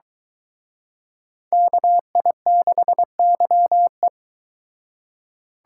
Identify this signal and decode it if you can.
Morse code, heard as a keyed tone, about 23 wpm, 700 Hz. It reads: KI6YE